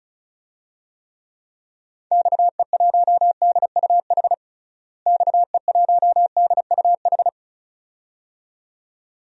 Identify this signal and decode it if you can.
Morse code, heard as a keyed tone, about 34 wpm, 700 Hz. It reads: XE1DUH XE1DUH
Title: DXCC-100 Call Signs - 6 Characters - Repeated 2x - 35wpm